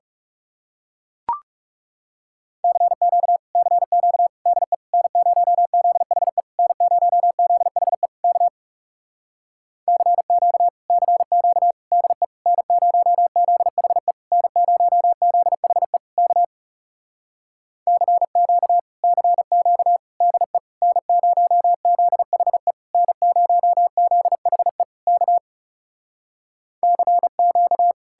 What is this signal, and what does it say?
Morse code, heard as a keyed tone, about 40 wpm, 700 Hz. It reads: CQ CQ DE N0ZHE N0ZHE K CQ CQ DE N0ZHE N0ZHE K CQ CQ DE N0ZHE N0ZHE K CQ